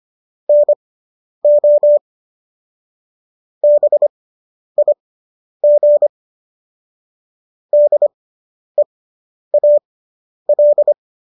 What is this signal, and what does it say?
Morse code, heard as a keyed tone, about 25 wpm, spaced as for 10 wpm, 590 Hz. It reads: NO BIG DEAL